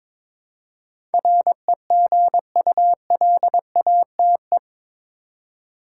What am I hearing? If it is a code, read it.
Morse code, heard as a keyed tone, about 22 wpm, 700 Hz. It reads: REGULATE
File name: practice / 22wpm / Eight Letter Words - 22wpm